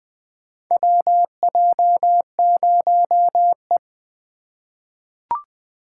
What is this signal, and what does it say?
Morse code, heard as a keyed tone, about 20 wpm, 700 Hz. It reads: WJ0E
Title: USA Call Signs - 4 Characters - 6x Character Spacing - 20wpm